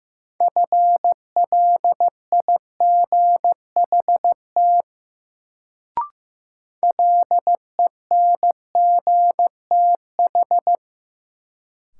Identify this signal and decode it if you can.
Morse code, heard as a keyed tone, about 15 wpm, 700 Hz. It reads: FLIGHT LENGTH
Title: Sentences from Top 1000 Words - New Word Drill - 15wpm